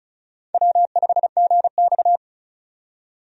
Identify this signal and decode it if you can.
Morse code, heard as a keyed tone, about 35 wpm, 700 Hz. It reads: W5GX
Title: USA Call Signs - 4 Characters - 35wpm